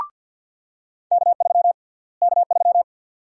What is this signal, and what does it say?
Morse code, heard as a keyed tone, about 47 wpm, 700 Hz. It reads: K3 K3
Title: USA Call Sign Prefixes - Repeated 2x - 50wpm